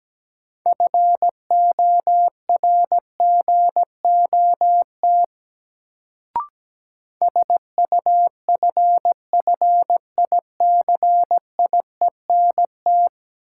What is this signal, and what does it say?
Morse code, heard as a keyed tone, about 17 wpm, 700 Hz. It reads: FORGOT SUFFICIENT
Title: Sentences from Top 1400 Words - New Word Drill - 17wpm